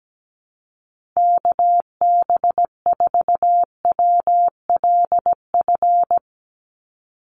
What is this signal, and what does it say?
Morse code, heard as a keyed tone, about 17 wpm, 700 Hz. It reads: KB4WLF
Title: Call Signs Repeated 3x - 17wpm